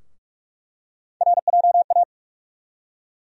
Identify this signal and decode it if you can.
Morse code, heard as a keyed tone, about 42 wpm, 700 Hz. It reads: RJA
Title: USA Call Sign Postfixes - 3 Letters 45wpm